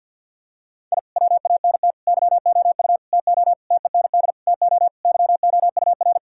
Morse code, heard as a keyed tone, about 51 wpm, 700 Hz. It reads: I WANT YOU TO TEND TO YOUR
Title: Sentences from Top 1000 Words - Review Part 5 - 50wpm